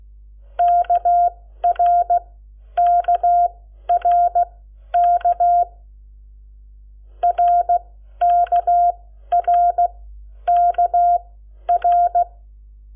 Morse code, heard as a keyed tone, about 16 wpm, spaced as for 11 wpm, 700 Hz. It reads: KRKRK RKRKR
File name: Lecciones / CLASE Nº7